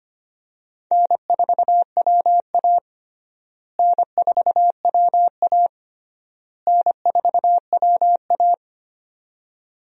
Morse code, heard as a keyed tone, about 25 wpm, 700 Hz. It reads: N4WA N4WA N4WA
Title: USA Call Signs - 4 Characters - Repeated 3x - 25wpm